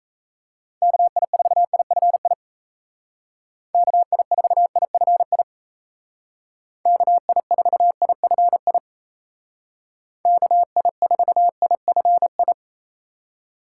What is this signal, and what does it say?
Morse code, heard as a keyed tone, about 35 wpm, 700 Hz. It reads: KI4IFI KI4IFI KI4IFI KI4IFI